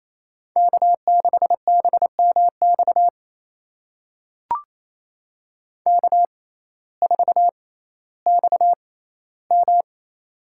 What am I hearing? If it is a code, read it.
Morse code, heard as a keyed tone, about 28 wpm, spaced as for 15 wpm, 700 Hz. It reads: K6BMX K4XM
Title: USA Call Signs - 6x Character Spacing - 28wpm